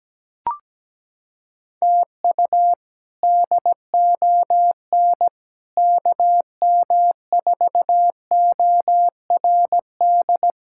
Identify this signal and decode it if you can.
Morse code, heard as a keyed tone, about 17 wpm, 700 Hz. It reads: TU DON KM4ORD
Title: K1USN SST QSO Practice - 17wpm